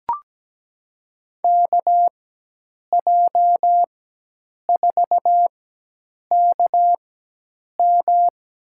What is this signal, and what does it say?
Morse code, heard as a keyed tone, about 17 wpm, spaced as for 8 wpm, 700 Hz. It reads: KJ4KM
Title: USA Call Signs - 4x Character Spacing - 17wpm